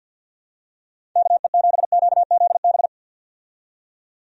Morse code, heard as a keyed tone, about 46 wpm, 700 Hz. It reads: KE7QZB
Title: USA Call Signs - 6 Characters - 50wpm